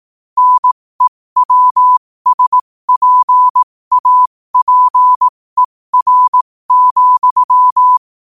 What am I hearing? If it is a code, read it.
Morse code, heard as a keyed tone, about 18 wpm, spaced as for 15 wpm, 1000 Hz. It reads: NEWSPAPER,
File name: qotdfiles / 15wpm